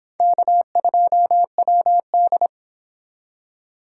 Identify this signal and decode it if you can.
Morse code, heard as a keyed tone, about 25 wpm, 700 Hz. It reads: K2WD